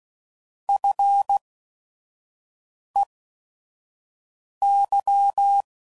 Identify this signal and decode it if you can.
Morse code, heard as a keyed tone, about 16 wpm, spaced as for 5 wpm, 800 Hz. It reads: FEY